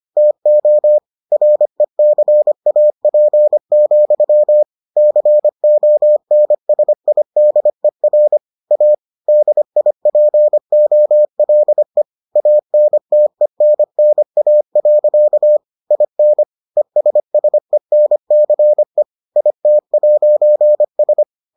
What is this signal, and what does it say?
Morse code, heard as a keyed tone, about 25 wpm, 600 Hz. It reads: TO RECAP, CONSIDER A DIPOLE ANTENNA. IN ESSENCE IT'S